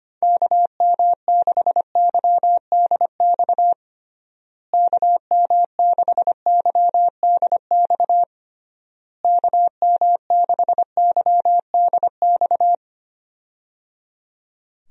Morse code, heard as a keyed tone, about 25 wpm, 700 Hz. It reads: KM6YDX KM6YDX KM6YDX
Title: USA Call Signs - 6 Characters - Repeated 3x - 25wpm